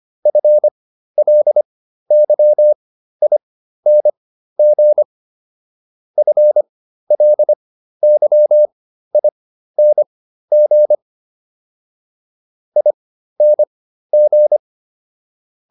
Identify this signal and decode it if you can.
Morse code, heard as a keyed tone, about 25 wpm, spaced as for 13 wpm, 600 Hz. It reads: FLYING FLYING ING